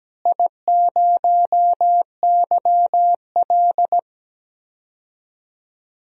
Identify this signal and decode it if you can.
Morse code, heard as a keyed tone, about 17 wpm, 700 Hz. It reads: I0YL
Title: DXCC-100 Call Signs - 4 Characters - Repeated 3x - 17wpm